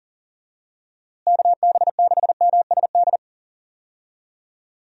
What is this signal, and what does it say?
Morse code, heard as a keyed tone, about 38 wpm, 700 Hz. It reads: KB6MSD
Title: USA Call Signs - 6 Characters - Repeated 3x - 40wpm